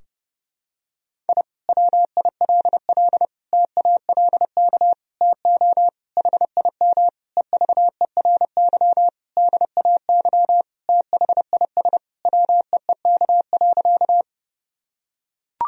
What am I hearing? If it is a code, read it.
Morse code, heard as a keyed tone, about 30 wpm, 700 Hz. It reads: I WILL TALK TO HIM EVERY DAY THIS WEEK.
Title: Sentences from Top 1000 Words - Review Part 2 - 30wpm